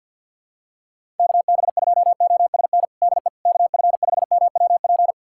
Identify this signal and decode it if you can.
Morse code, heard as a keyed tone, about 49 wpm, 700 Hz. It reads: KB2OSN DE KF5MWP
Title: USA Callsign DE Callsign 50wpm